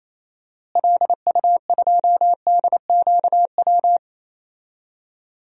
Morse code, heard as a keyed tone, about 28 wpm, 700 Hz. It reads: LU2DQW
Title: DXCC-100 Call Signs - 6 Characters - Repeated 3x - 28wpm